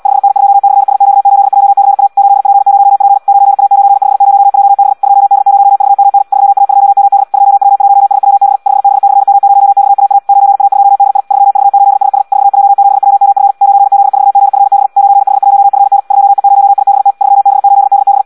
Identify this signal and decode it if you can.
Morse code, heard as a keyed tone, about 76 wpm, 800 Hz. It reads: 5N2/SQ8JCA 8P0P 9A950JB 2D0VMN 3A2MD 2W1SWB 5B/M0XAA 9A2DI 3V8SS 4J7WMF 9Z4GFB 9H1FN 2E0IBT 3B8DB
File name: Audio practica CW / qrq-tx-78 2